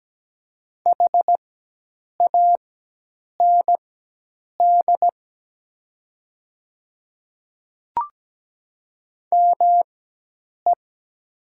Morse code, heard as a keyed tone, about 17 wpm, spaced as for 8 wpm, 700 Hz. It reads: HAND ME
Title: 4x Character Spacing - Top 500 Words - 17wpm